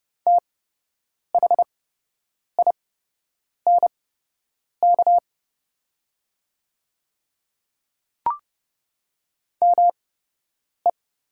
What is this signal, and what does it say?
Morse code, heard as a keyed tone, about 30 wpm, spaced as for 8 wpm, 700 Hz. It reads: THINK ME
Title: 8x Character Spacing - Top 300 Words - 30wpm